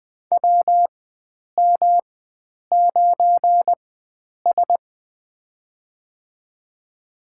Morse code, heard as a keyed tone, about 20 wpm, spaced as for 9 wpm, 700 Hz. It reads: WM9S
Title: USA Call Signs - 4x Character Spacing - 20wpm